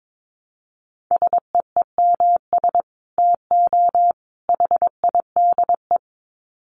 Morse code, heard as a keyed tone, about 22 wpm, 700 Hz. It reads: SEEMS TO HIDE
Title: Dit Challenge 22wpm